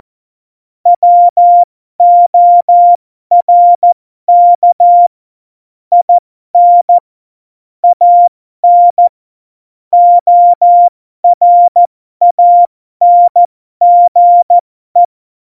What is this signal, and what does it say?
Morse code, heard as a keyed tone, about 14 wpm, spaced as for 12 wpm, 700 Hz. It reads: WORK IN AN ORANGE
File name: Morsekurs / Witze